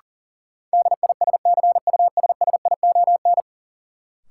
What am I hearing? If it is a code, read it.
Morse code, heard as a keyed tone, about 40 wpm, 700 Hz. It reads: DISCUSSION
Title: Sentences from Top 1100 Words - New Word Drill - 40wpm